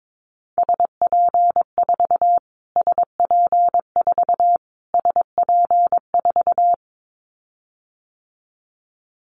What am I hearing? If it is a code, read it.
Morse code, heard as a keyed tone, about 22 wpm, 700 Hz. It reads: SP4 SP4 SP4
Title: Callsign Prefixes for Top 25 DXCC - Repeated 3x - 22wpm